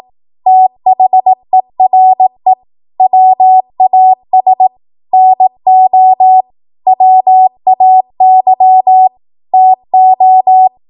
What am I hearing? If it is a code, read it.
Morse code, heard as a keyed tone, about 18 wpm, 750 Hz. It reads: THERE WAS NO WAY TO
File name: Morse / 18WPM